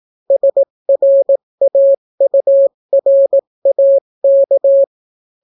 Morse code, heard as a keyed tone, about 18 wpm, 550 Hz. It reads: SRAURAK